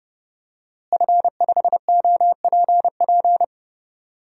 Morse code, heard as a keyed tone, about 30 wpm, 700 Hz. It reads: F5OPP
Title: DXCC-100 Call Signs - 5 Characters - Repeated 2x - 30wpm